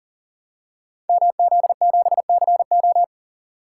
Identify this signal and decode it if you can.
Morse code, heard as a keyed tone, about 42 wpm, 690 Hz. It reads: MZ7CO
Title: Code Groups - 5 Letter:Numbers - 40wpm